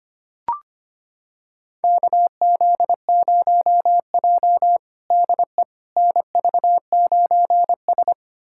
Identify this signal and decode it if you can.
Morse code, heard as a keyed tone, about 25 wpm, 700 Hz. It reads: KZ0J DE NV9S
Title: USA Callsign DE Callsign - 4 Characters - 25wpm